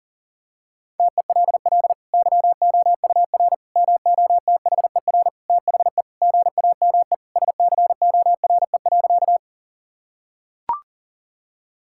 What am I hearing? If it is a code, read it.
Morse code, heard as a keyed tone, about 40 wpm, 700 Hz. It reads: TELL YOUR MOTHER THE GAME SCORE.
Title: Sentences from Top 1600 Words 40wpm